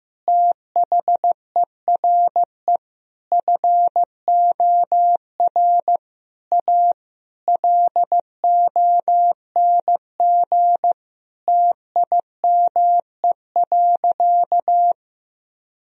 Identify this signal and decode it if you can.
Morse code, heard as a keyed tone, about 15 wpm, 700 Hz. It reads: THERE FOR A LONG TIME.
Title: Sentences from Top 1000 Words - Review Part 2 - 15wpm